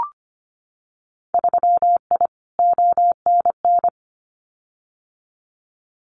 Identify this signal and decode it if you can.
Morse code, heard as a keyed tone, about 25 wpm, 700 Hz. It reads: 3I ONN
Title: Winter Field Day Exchanges Encore 25wpm